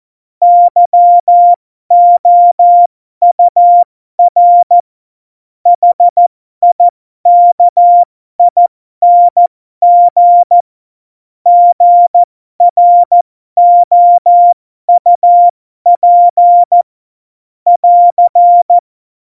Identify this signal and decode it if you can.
Morse code, heard as a keyed tone, about 14 wpm, spaced as for 12 wpm, 700 Hz. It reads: YOUR HIKING GROUP <AR>
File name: Morsekurs / Witze